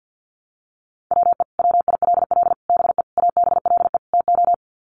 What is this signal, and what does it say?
Morse code, heard as a keyed tone, about 50 wpm, 700 Hz. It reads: WE WILL BE ABLE TO
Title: Sets of 5 Words Encore 50wpm